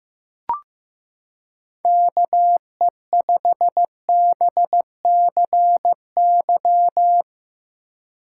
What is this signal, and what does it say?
Morse code, heard as a keyed tone, about 15 wpm, 700 Hz. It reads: KE5BCY
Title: K1USN SST QSO Practice - 15wpm